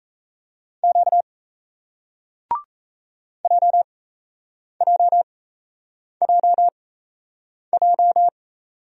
Morse code, heard as a keyed tone, about 36 wpm, spaced as for 8 wpm, 700 Hz. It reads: Q JJJJ